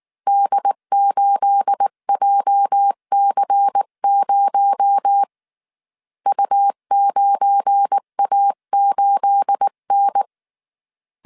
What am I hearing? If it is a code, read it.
Morse code, heard as a keyed tone, about 19 wpm, 800 Hz. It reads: D8JC0 U9A8N